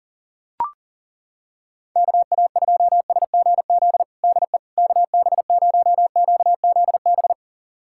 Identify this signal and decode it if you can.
Morse code, heard as a keyed tone, about 40 wpm, 700 Hz. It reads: KA2SGZ DE KB0QZB
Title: USA Callsign DE Callsign 40wpm